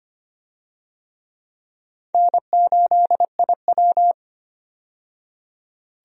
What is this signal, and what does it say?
Morse code, heard as a keyed tone, about 25 wpm, 700 Hz. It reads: N8IW
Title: USA Call Signs - 4 Characters - 25wpm